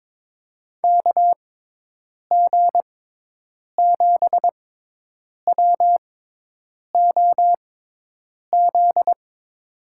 Morse code, heard as a keyed tone, about 22 wpm, spaced as for 8 wpm, 700 Hz. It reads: KG7WOZ
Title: USA Call Signs - 6 Characters - 6x Character Spacing - 22wpm